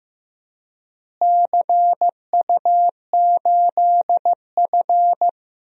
Morse code, heard as a keyed tone, about 15 wpm, 700 Hz. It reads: CU8F
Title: DXCC-100 Call Signs - 4 Characters - Repeated 2x - 15wpm